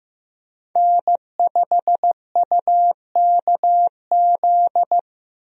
Morse code, heard as a keyed tone, about 15 wpm, 700 Hz. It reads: N5UKZ
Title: USA Call Signs - 5 Characters - Repeated 2x - 15wpm